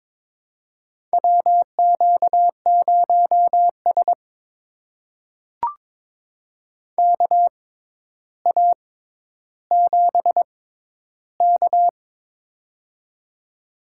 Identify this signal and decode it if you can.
Morse code, heard as a keyed tone, about 22 wpm, spaced as for 11 wpm, 700 Hz. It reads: WQ0S KA7K
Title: USA Call Signs - 4 Characters - 6x Character Spacing - 22wpm